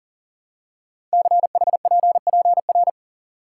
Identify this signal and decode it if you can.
Morse code, heard as a keyed tone, about 38 wpm, 700 Hz. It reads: CHPPR
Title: Code Groups - 5 Letters - 40wpm